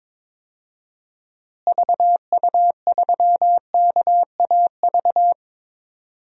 Morse code, heard as a keyed tone, about 22 wpm, 700 Hz. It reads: VU3KAV
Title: DXCC-100 Call Signs - 6 Characters - Repeated 3x - 22wpm